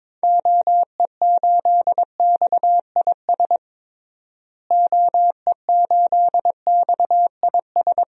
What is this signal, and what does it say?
Morse code, heard as a keyed tone, about 22 wpm, 700 Hz. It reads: OE8XIS OE8XIS
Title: DXCC-100 Call Signs - 6 Characters - Repeated 3x - 22wpm